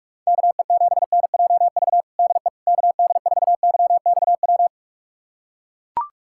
Morse code, heard as a keyed tone, about 42 wpm, 700 Hz. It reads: KE7NJV DE KD4YXW
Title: USA Callsign DE Callsign 45wpm